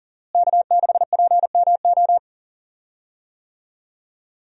Morse code, heard as a keyed tone, about 40 wpm, 700 Hz. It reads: K6PMO
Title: USA Call Signs - 5 Characters - Repeated 2x - 40wpm